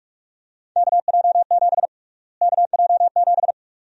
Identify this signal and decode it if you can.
Morse code, heard as a keyed tone, about 46 wpm, 700 Hz. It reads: KJ7 KJ7
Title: USA Call Sign Prefixes - Repeated 2x - 45wpm